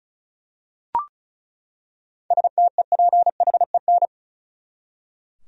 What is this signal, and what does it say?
Morse code, heard as a keyed tone, about 36 wpm, 700 Hz. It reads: STEPHEN
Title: USA Common Names - 35wpm